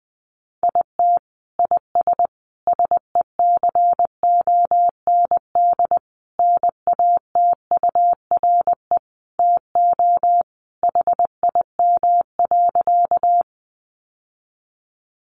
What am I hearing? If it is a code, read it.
Morse code, heard as a keyed tone, about 19 wpm, 700 Hz. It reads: IT IS SECOND NATURE TO HIM.
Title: Sentences from Top 600 Words 20wpm